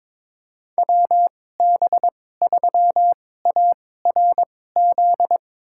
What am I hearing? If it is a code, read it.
Morse code, heard as a keyed tone, about 22 wpm, spaced as for 16 wpm, 700 Hz. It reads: WB3ARZ
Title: USA Call Signs - 6 Characters - 2x Character Spacing - 22wpm